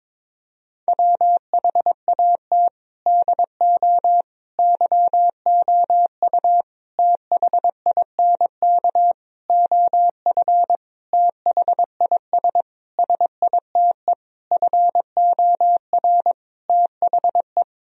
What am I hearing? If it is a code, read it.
Morse code, heard as a keyed tone, about 22 wpm, 700 Hz. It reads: WHAT DO YOU THINK OF THIS SITE FOR THE
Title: Sentences from Top 1000 Words - Review Part 1 - 22wpm